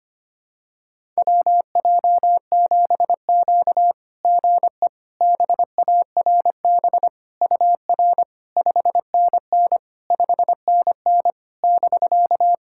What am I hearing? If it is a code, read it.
Morse code, heard as a keyed tone, about 25 wpm, 700 Hz. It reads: WJ7Q GE BARB UR 5NN 5NN <BK>
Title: SOTA QSO Practice 25wpm